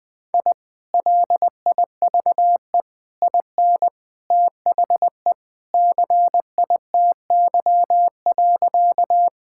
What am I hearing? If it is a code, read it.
Morse code, heard as a keyed tone, about 20 wpm, 700 Hz. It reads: I LIVE IN THE CITY.
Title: Sentences with 2-Word Phrases - Spoken Beginning - 20wpm